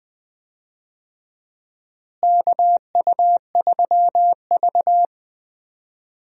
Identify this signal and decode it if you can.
Morse code, heard as a keyed tone, about 20 wpm, 700 Hz. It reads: KU3V
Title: USA Call Signs - 4 Characters - Repeated 2x - 20wpm